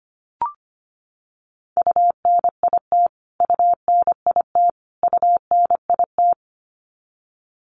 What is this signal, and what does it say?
Morse code, heard as a keyed tone, about 25 wpm, 700 Hz. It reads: UNIT UNIT UNIT
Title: Top 500 Words Repeated 25wpm